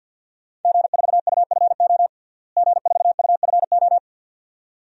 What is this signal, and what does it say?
Morse code, heard as a keyed tone, about 51 wpm, 700 Hz. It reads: G4UFO G4UFO
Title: DXCC-100 Call Signs - 5 Characters - Repeated 2x - 50wpm